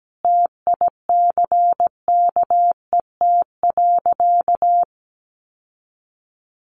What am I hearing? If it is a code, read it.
Morse code, heard as a keyed tone, about 17 wpm, 700 Hz. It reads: TICKET.
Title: Sentences from Top 1400 Words 17wpm